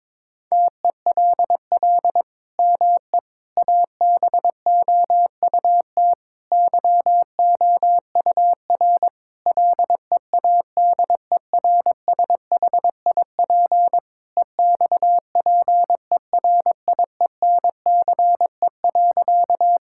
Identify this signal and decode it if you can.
Morse code, heard as a keyed tone, about 22 wpm, 700 Hz. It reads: TELL ME ABOUT YOUR LEADERSHIP EXPERIENCE.
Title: Sentences from Top 1600 Words 22wpm